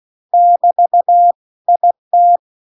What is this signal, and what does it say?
Morse code, heard as a keyed tone, about 16 wpm, 700 Hz. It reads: <BT> IT